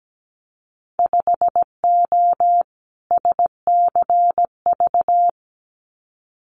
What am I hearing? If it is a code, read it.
Morse code, heard as a keyed tone, about 17 wpm, 700 Hz. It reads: 5O SCV
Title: Winter Field Day Exchanges Encore 17wpm